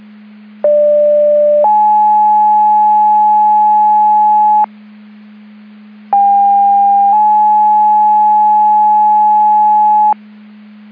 Two-tone page, heard as a one-second tone, then a three-second tone, about 590 then 850 Hz, about 800 then 850 Hz. It is heard from 0.6 to 10.1 s.